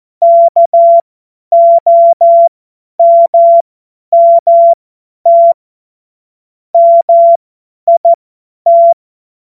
Morse code, heard as a keyed tone, about 14 wpm, spaced as for 10 wpm, 680 Hz. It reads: KOMMT MIT